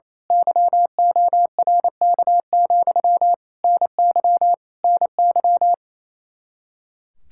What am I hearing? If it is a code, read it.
Morse code, heard as a keyed tone, about 28 wpm, 700 Hz. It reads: YORK, NY NY
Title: Speed-Racing Common QSO Phrases - 28wpm